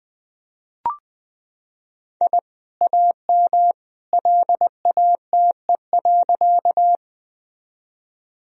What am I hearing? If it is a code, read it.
Morse code, heard as a keyed tone, about 20 wpm, 700 Hz. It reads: I AM LATE.
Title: Sentences from Top 1000 Words - Review Part 2 - 20wpm